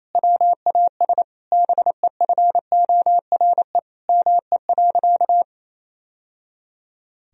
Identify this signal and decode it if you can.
Morse code, heard as a keyed tone, about 28 wpm, 700 Hz. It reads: WAS BEFORE ME.
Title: Sentences from Top 1000 Words - Review Part 1 - 28wpm